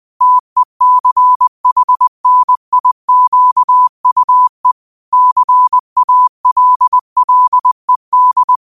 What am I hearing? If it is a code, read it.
Morse code, heard as a keyed tone, about 20 wpm, 1000 Hz. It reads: TECHNIQUE CALLED
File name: qotdfiles / 20wpm